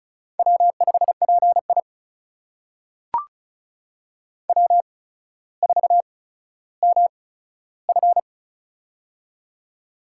Morse code, heard as a keyed tone, about 36 wpm, spaced as for 15 wpm, 700 Hz. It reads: W5PI W4MF
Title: USA Call Signs - 8x Character Spacing - 35wpm